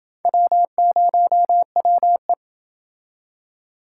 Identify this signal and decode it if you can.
Morse code, heard as a keyed tone, about 29 wpm, 700 Hz. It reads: W0WE